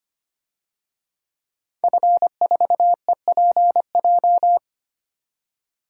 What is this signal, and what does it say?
Morse code, heard as a keyed tone, about 25 wpm, 700 Hz. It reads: F4EPJ
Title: DXCC-100 Call Signs - 5 Characters - 25wpm